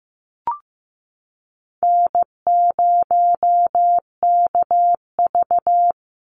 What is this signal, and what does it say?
Morse code, heard as a keyed tone, about 15 wpm, 700 Hz. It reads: N0KV
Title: Call Signs Repeated 3x - 15wpm